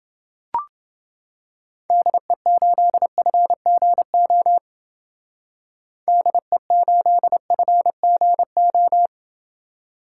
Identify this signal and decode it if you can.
Morse code, heard as a keyed tone, about 29 wpm, 700 Hz. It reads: DE8FGO DE8FGO